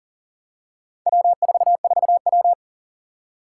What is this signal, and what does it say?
Morse code, heard as a keyed tone, about 38 wpm, 700 Hz. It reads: W44W
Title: Four Challenging Letter-Number Combinations 40wpm